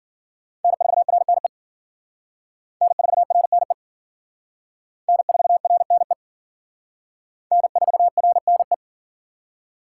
Morse code, heard as a keyed tone, about 50 wpm, 700 Hz. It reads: N4RNE N4RNE N4RNE N4RNE